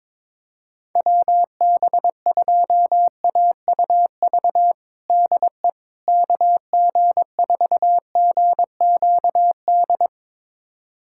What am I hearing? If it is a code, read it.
Morse code, heard as a keyed tone, about 21 wpm, 700 Hz. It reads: WB2AUV DE KG4GQD
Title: USA Callsign DE Callsign - 6 Characters - 22wpm